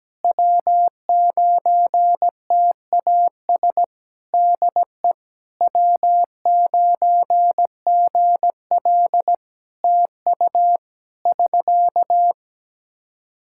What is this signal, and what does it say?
Morse code, heard as a keyed tone, about 17 wpm, 700 Hz. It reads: W9TAS DE W9GL TU <SK>
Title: Practice QSO - Exchange 4 - Ending - 17wpm